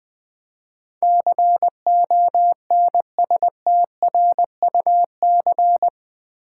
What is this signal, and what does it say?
Morse code, heard as a keyed tone, about 20 wpm, 700 Hz. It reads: CONSTRUC
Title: Build A Word - 4 Letters At A Time - 12-Letter Words - 20wpm